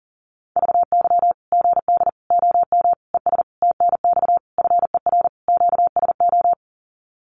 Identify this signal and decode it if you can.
Morse code, heard as a keyed tone, about 40 wpm, 700 Hz. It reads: VY GD OM ES TNX FER QSO
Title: Common QSO Phrases 40wpm